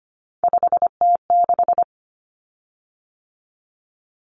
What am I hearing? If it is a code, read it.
Morse code, heard as a keyed tone, about 25 wpm, 700 Hz. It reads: 5T6
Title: Common Cut Numbers - 3-Digit 25wpm